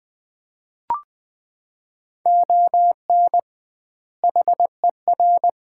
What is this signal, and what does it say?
Morse code, heard as a keyed tone, about 20 wpm, 700 Hz. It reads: ON HER
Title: 2x Word-Spacing - Sets of 2 Words - 20wpm